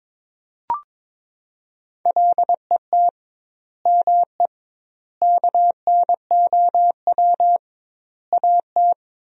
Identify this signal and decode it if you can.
Morse code, heard as a keyed tone, about 22 wpm, 700 Hz. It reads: LET ME KNOW AT
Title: 2x Word-Spacing - Sentences from Top 300 Words - 22wpm